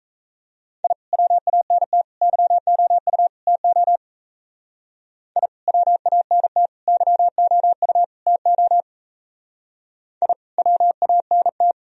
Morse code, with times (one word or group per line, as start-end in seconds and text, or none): 0.8-0.9 I
1.1-2.0 WANT
2.2-3.3 YOU
3.5-4.0 TO
5.4-5.5 I
5.7-6.7 WANT
6.9-8.0 YOU
8.3-8.8 TO
10.2-10.3 I
10.6-11.7 WANT